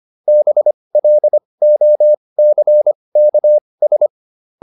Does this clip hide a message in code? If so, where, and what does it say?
Yes: Morse code from 0.3 to 4.1 s, heard as a keyed tone, about 25 wpm, spaced as for 20 wpm, 600 Hz. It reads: BLOCKS